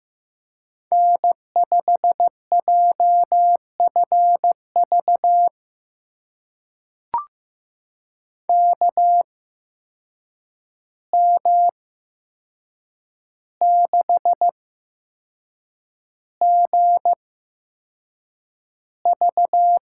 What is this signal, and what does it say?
Morse code, heard as a keyed tone, about 15 wpm, spaced as for 6 wpm, 700 Hz. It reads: N5JFV KM6GV
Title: USA Call Signs - 5 Characters - 8x Character Spacing - 15wpm